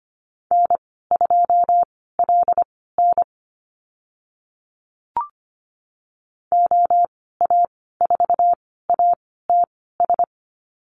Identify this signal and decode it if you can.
Morse code, heard as a keyed tone, about 25 wpm, spaced as for 16 wpm, 700 Hz. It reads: N2LN OA4ATS
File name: Lesson 15g - Number 5 - Call Signs 25wpm